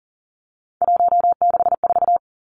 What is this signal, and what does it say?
Morse code, heard as a keyed tone, about 40 wpm, 700 Hz. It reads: 164
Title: Numbers - 3-Digit 40wpm